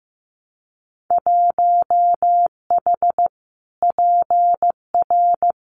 Morse code, heard as a keyed tone, about 15 wpm, 700 Hz. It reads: 1H PR
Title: Winter Field Day Exchanges Encore 15wpm